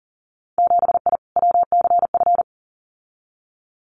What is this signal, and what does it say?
Morse code, heard as a keyed tone, about 40 wpm, 700 Hz. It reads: 7I WCF
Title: Winter Field Day Exchanges Encore 40wpm